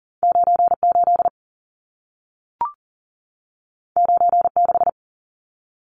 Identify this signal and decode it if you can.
Morse code, heard as a keyed tone, about 40 wpm, 700 Hz. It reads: 98 96
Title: Numbers - 2-Digit 40wpm